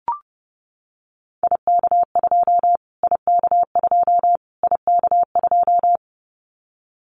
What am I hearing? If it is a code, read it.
Morse code, heard as a keyed tone, about 30 wpm, 700 Hz. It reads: IK2 IK2 IK2
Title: Callsign Prefixes for Top 25 DXCC - Repeated 3x - 30wpm